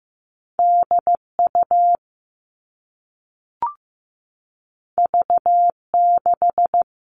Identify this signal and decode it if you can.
Morse code, heard as a keyed tone, about 15 wpm, 700 Hz. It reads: DU V6
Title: All Cut Numbers - 2-Digit 15wpm